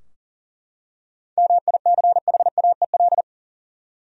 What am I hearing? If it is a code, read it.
Morse code, heard as a keyed tone, about 40 wpm, 700 Hz. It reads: MICHAEL
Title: USA Common Names - 40wpm